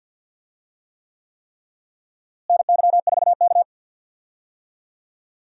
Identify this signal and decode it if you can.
Morse code, heard as a keyed tone, about 52 wpm, 700 Hz. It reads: NY4K
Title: USA Call Signs - 4 Characters - Repeated 3x - 50wpm